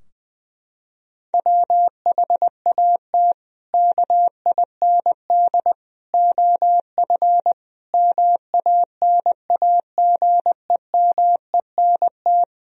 Morse code, heard as a keyed tone, about 20 wpm, 700 Hz. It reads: WHAT KIND OF MANAGEMENT
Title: Sentences from Top 1000 Words - Review Part 4 - 20wpm